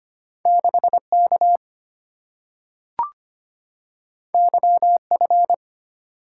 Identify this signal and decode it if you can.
Morse code, heard as a keyed tone, about 25 wpm, 700 Hz. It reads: 6K YF
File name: Two Letter-Number Combinations - 25wpm